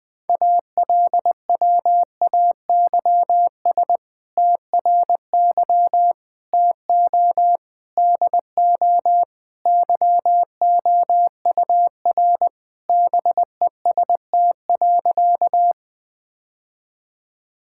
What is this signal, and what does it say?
Morse code, heard as a keyed tone, about 21 wpm, 700 Hz. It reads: ALWAYS TRY TO DO YOUR BEST.
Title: Sentences with 2-Word Phrases - 20wpm